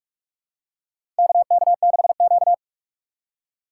Morse code, heard as a keyed tone, about 43 wpm, 700 Hz. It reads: KK6Q
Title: USA Call Signs - 4 Characters - 4x Character Spacing - 45wpm